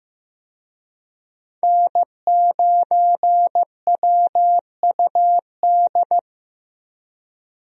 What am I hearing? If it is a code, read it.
Morse code, heard as a keyed tone, about 15 wpm, 700 Hz. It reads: N9WUD
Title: USA Call Signs - 5 Characters - Repeated 3x - 15wpm